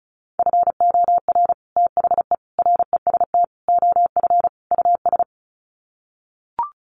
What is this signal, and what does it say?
Morse code, heard as a keyed tone, about 34 wpm, 700 Hz. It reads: FOR THE REST OF US
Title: Sets of 5 Words Encore 35wpm